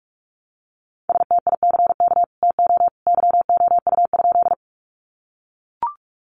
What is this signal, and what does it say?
Morse code, heard as a keyed tone, about 44 wpm, 700 Hz. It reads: STICK TO YOU?
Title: Sentences from Top 1200 Words 45wpm